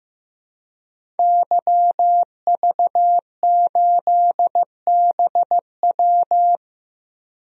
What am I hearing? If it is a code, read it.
Morse code, heard as a keyed tone, about 15 wpm, 700 Hz. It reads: YV8BW
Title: DXCC-100 Call Signs - 5 Characters - Repeated 2x - 15wpm